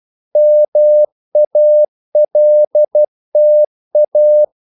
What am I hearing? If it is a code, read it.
Morse code, heard as a keyed tone, about 12 wpm, 600 Hz. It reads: MALTA
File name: CW / Practice / NationalCapitals12